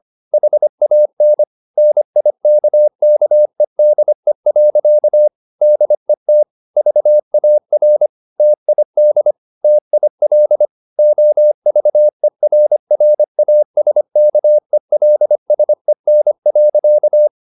Morse code, heard as a keyed tone, about 25 wpm, 600 Hz. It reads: HAN NIKKEDE. DET VAR TID TIL OVERRASKELSEN.
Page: Ferie 25 wpm